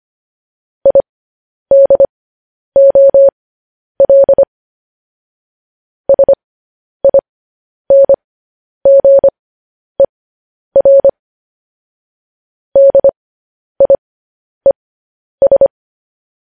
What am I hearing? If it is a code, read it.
Morse code, heard as a keyed tone, about 25 wpm, spaced as for 10 wpm, 550 Hz. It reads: IDOL SINGER DIES